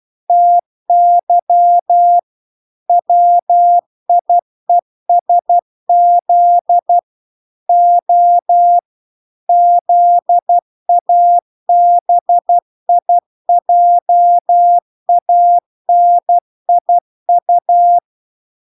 Morse code, heard as a keyed tone, about 12 wpm, 700 Hz. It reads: TY WIESZ O ZABIJANIU